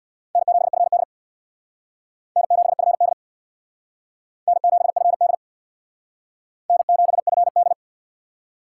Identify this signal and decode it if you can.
Morse code, heard as a keyed tone, about 61 wpm, 700 Hz. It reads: N7FD N7FD N7FD N7FD
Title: Speed Racing - USA 1x2 Call Signs 50wpm